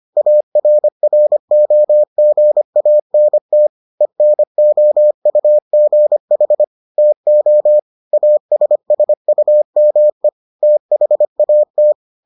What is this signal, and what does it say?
Morse code, heard as a keyed tone, about 25 wpm, 600 Hz. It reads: ARROGANT ENOUGH TO ASSUME THAT